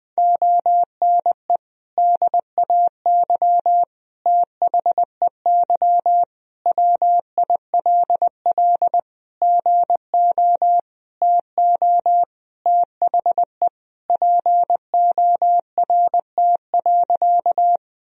Morse code, heard as a keyed tone, about 21 wpm, 700 Hz. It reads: ONE DAY THEY WILL GO TO THE PORT.
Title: Sentences from Top 1000 Words - Review Part 1 - 20wpm